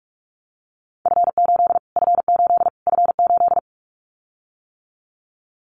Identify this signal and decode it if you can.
Morse code, heard as a keyed tone, about 44 wpm, 700 Hz. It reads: F8 F8 F8
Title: Callsign Prefixes for Top 25 DXCC - Repeated 3x - 45wpm